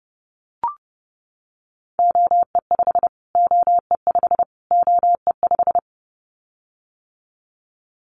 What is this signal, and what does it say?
Morse code, heard as a keyed tone, about 28 wpm, 700 Hz. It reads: OE5 OE5 OE5
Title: Callsign Prefixes for Top 25 DXCC - Repeated 3x - 30wpm